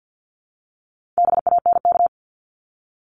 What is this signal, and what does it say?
Morse code, heard as a keyed tone, about 50 wpm, 700 Hz. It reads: BANK
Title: Top 751-1000 Words 50wpm